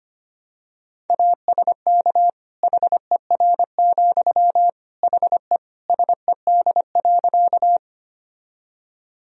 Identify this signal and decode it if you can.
Morse code, heard as a keyed tone, about 25 wpm, 700 Hz. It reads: ASK HER, HE SED.
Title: Sentences from Top 200 Words and Abbreviations 25wpm